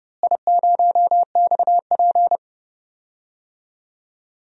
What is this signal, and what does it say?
Morse code, heard as a keyed tone, about 32 wpm, 700 Hz. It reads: I0XP
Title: DXCC-100 Call Signs - 4 Characters - Repeated 3x - 30wpm